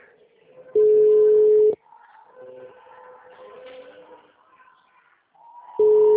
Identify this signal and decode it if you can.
Ringback tone (Europe, 425 Hz), with speech.